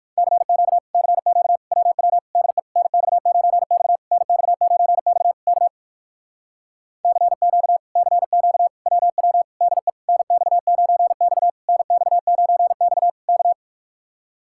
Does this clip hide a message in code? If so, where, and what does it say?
Yes: Morse code from 0.2 to 13.5 s, heard as a keyed tone, about 67 wpm, spaced as for 57 wpm, 700 Hz. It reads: CQ CQ WW DE NX9X NX9X K CQ CQ WW DE NX9X NX9X K